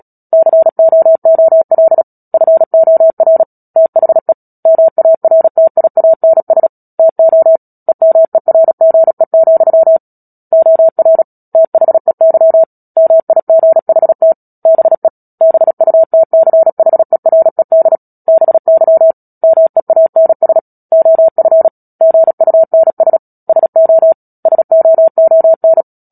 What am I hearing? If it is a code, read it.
Morse code, heard as a keyed tone, about 37 wpm, 650 Hz. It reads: COOL FOR THE MARTIANS TO EMERGE, OR THEY MIGHT BE BUTCHERED BY MEANS OF GUNS SO SOON
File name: Book Two / 10-Chapter 10 - Epilogue - 36 WPM